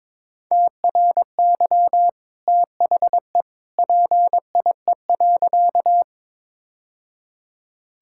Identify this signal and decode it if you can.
Morse code, heard as a keyed tone, about 22 wpm, 700 Hz. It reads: TRY THE PIE.
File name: practice / 22wpm / Easy Sentences with Hints - 22wpm